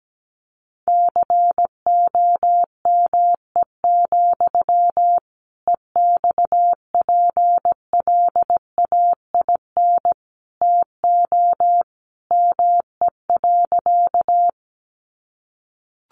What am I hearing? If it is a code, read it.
Morse code, heard as a keyed tone, about 17 wpm, 700 Hz. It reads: COME, EXPLAIN TO ME.
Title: Sentences from Top 1100 Words 17wpm